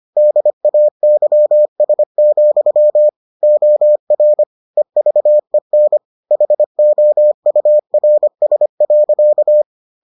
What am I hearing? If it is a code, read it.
Morse code, heard as a keyed tone, about 25 wpm, 600 Hz. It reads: DAYS, OR EVEN HOURS.